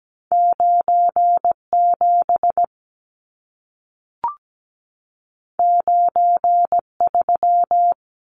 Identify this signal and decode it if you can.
Morse code, heard as a keyed tone, about 17 wpm, 700 Hz. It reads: 97 93
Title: Numbers - 2-Digit 17wpm